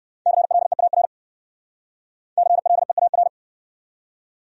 Morse code, heard as a keyed tone, about 67 wpm, 700 Hz. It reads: CLEAR CLEAR